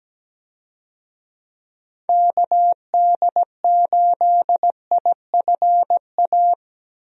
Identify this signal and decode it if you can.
Morse code, heard as a keyed tone, about 17 wpm, 700 Hz. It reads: KD8IFA
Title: USA Call Signs - 6 Characters - Repeated 3x - 17wpm